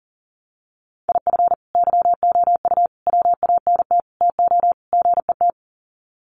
Morse code, heard as a keyed tone, about 40 wpm, 700 Hz. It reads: IF YOU WANT TO GET
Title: Sets of 5 Words Encore 40wpm